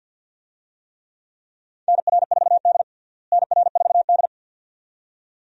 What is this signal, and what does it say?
Morse code, heard as a keyed tone, about 46 wpm, 700 Hz. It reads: NR4D NR4D
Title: USA Call Signs - 4 Characters - Repeated 2x - 50wpm